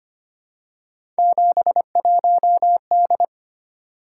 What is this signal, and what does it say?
Morse code, heard as a keyed tone, about 25 wpm, 700 Hz. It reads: 71D
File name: Three Letter-Number Combinations - 25wpm